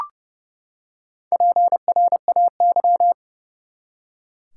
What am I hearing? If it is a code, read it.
Morse code, heard as a keyed tone, about 30 wpm, 700 Hz. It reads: PRAY
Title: Four Letter Words - 30wpm